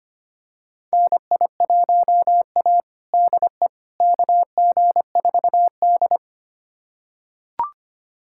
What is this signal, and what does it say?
Morse code, heard as a keyed tone, about 25 wpm, 700 Hz. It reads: NI1A DE KG4D
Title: USA Callsign DE Callsign - 4 Characters - 25wpm